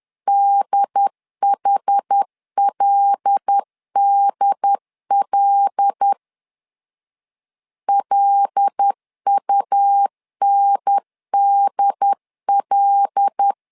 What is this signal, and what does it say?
Morse code, heard as a keyed tone, about 10 wpm, 800 Hz. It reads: DHLDL LUNDL
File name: LEZIONI / 5 LBVC / 5 L 10wpm 2259